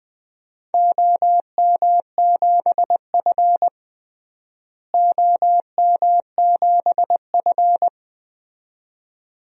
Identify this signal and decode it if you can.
Morse code, heard as a keyed tone, about 20 wpm, 700 Hz. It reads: OM7F OM7F
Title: DXCC-100 Call Signs - 4 Characters - Repeated 2x - 20wpm